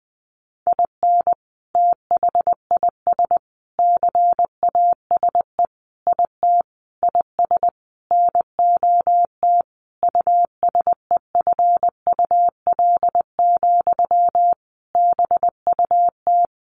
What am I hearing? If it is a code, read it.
Morse code, heard as a keyed tone, about 20 wpm, 700 Hz. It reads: IN THIS CASE IT IS NOT USEFUL, BUT
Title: Sentences from Top 1400 Words 20wpm